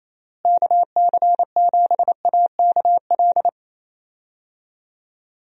Morse code, heard as a keyed tone, about 28 wpm, 700 Hz. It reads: KC7AKL
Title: USA Call Signs - 6 Characters - Repeated 2x - 28wpm